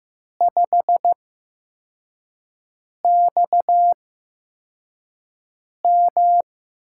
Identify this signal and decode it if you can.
Morse code, heard as a keyed tone, about 15 wpm, spaced as for 4 wpm, 700 Hz. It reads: 5XM
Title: USA Call Signs - 8x Character Spacing - 15wpm